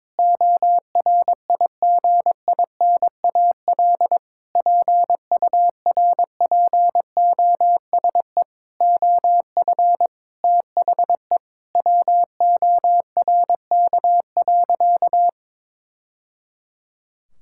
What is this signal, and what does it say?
Morse code, heard as a keyed tone, about 22 wpm, 700 Hz. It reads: ORIGINAL PURPOSE OF THE WORK.
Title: Sentences from Top 1000 Words - Review Part 6 - 22wpm